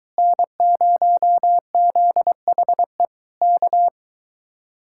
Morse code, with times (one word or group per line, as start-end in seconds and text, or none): 0.2-3.0 N0ZHE
3.4-3.9 K